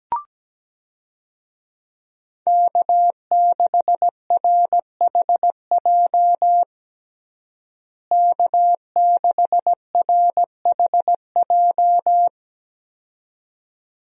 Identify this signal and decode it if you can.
Morse code, heard as a keyed tone, about 17 wpm, 700 Hz. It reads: K6RHJ K6RHJ
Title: USA Call Signs - 5 Characters - Repeated 2x - 17wpm